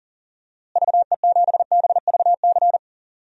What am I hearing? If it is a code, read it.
Morse code, heard as a keyed tone, about 37 wpm, 700 Hz. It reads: VE7BVC
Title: DXCC-100 Call Signs - 6 Characters - Repeated 2x - 40wpm